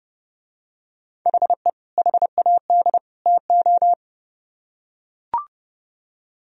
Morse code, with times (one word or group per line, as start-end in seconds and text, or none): 1.3-1.7 HE
2.0-3.0 HAD
3.3-3.9 TO